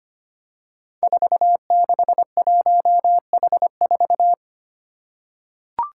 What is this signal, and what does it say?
Morse code, heard as a keyed tone, about 25 wpm, 700 Hz. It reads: 461H4
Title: Five Challenging Letter-Number Combinations 25wpm